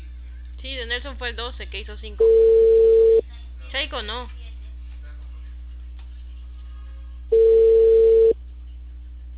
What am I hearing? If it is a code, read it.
Ringback tone (China, 450 Hz), with speech.